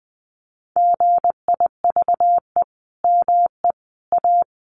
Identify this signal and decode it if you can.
Morse code, heard as a keyed tone, about 20 wpm, 700 Hz. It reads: GIVE ME A
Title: Sentences from Top 200 Words 20wpm